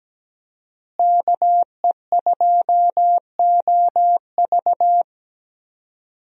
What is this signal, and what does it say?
Morse code, heard as a keyed tone, about 17 wpm, 700 Hz. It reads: KE2OV
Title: USA Call Signs - 5 Characters - Repeated 3x - 17wpm